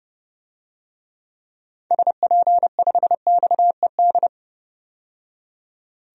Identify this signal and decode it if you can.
Morse code, heard as a keyed tone, about 32 wpm, 700 Hz. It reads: SP5XED
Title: DXCC-100 Call Signs - 6 Characters - 30wpm